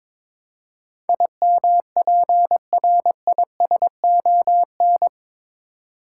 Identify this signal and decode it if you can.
Morse code, heard as a keyed tone, about 22 wpm, 700 Hz. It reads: IMPRISON
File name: practice / 22wpm / Eight Letter Words - 22wpm